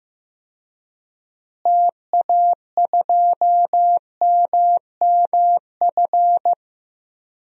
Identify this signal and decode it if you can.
Morse code, heard as a keyed tone, about 15 wpm, 700 Hz. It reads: TA2MMF
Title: DXCC-100 Call Signs - 6 Characters - Repeated 3x - 15wpm